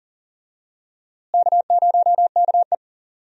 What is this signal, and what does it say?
Morse code, heard as a keyed tone, about 40 wpm, 700 Hz. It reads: K0KE
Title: USA Call Signs - 4 Characters - Repeated 2x - 40wpm